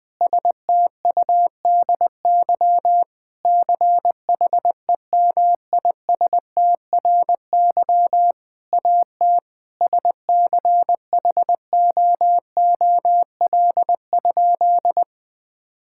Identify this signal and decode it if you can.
Morse code, heard as a keyed tone, about 21 wpm, 700 Hz. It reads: STUDY CHEMISTRY AT SCHOOL?
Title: Sentences from Top 1000 Words - Review Part 6 - 20wpm